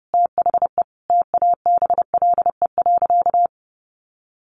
Morse code, heard as a keyed tone, about 28 wpm, 700 Hz. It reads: THE TABLE.
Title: Sentences from Top 1300 Words 30wpm